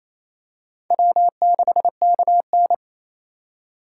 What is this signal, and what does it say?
Morse code, heard as a keyed tone, about 28 wpm, 700 Hz. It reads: W6KN
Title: USA Call Signs - 3x Character Spacing - 28wpm